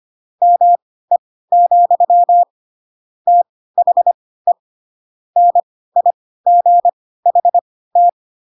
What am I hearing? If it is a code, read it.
Morse code, heard as a keyed tone, about 25 wpm, spaced as for 16 wpm, 700 Hz. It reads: ME, THE NIGHT
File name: Pinocchio / Pinocchio ch7 10 25 ik1hge